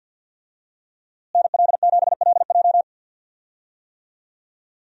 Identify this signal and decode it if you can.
Morse code, heard as a keyed tone, about 50 wpm, 700 Hz. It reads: NL7LJ
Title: USA Call Signs - 5 Characters - Repeated 2x - 50wpm